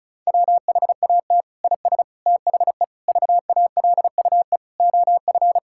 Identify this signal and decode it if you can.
Morse code, heard as a keyed tone, about 36 wpm, 700 Hz. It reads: WHAT IS THE VALUE OF
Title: Sentences from Top 1000 Words - Review Part 6 - 35wpm